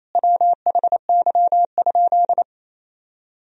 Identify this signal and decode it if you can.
Morse code, heard as a keyed tone, about 28 wpm, 700 Hz. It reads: WHY?
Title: Sentences from Top 200 Words and Abbreviations 28wpm